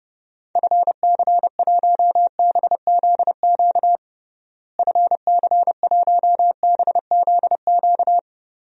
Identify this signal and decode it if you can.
Morse code, heard as a keyed tone, about 32 wpm, 700 Hz. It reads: FC1BZQ FC1BZQ
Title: DXCC-100 Call Signs - 6 Characters - Repeated 2x - 30wpm